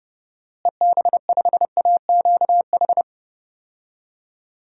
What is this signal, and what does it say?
Morse code, heard as a keyed tone, about 30 wpm, 700 Hz. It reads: EB5AQH
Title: DXCC-100 Call Signs - 6 Characters - Repeated 3x - 30wpm